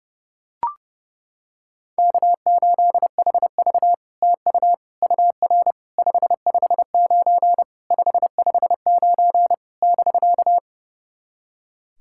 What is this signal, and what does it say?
Morse code, heard as a keyed tone, about 30 wpm, 690 Hz. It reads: K8HV TU UR 559 559 <BK>
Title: SOTA QSO Practice 30wpm